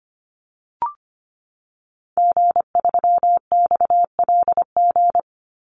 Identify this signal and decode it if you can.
Morse code, heard as a keyed tone, about 25 wpm, 700 Hz. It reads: G3XLG
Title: Call Signs - 25wpm